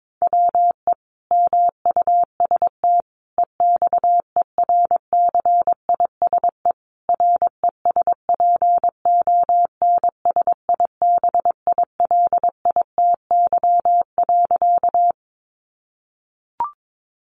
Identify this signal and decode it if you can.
Morse code, heard as a keyed tone, about 22 wpm, 700 Hz. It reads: WE MUST EXERCISE RESPONSIBILITY.
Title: Sentences from Top 1200 Words 22wpm